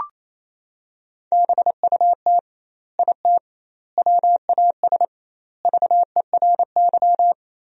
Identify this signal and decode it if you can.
Morse code, heard as a keyed tone, about 27 wpm, 700 Hz. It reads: BUT IT WAS VERY
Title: 2x Word-Spacing - Sentences from Top 300 Words - 28wpm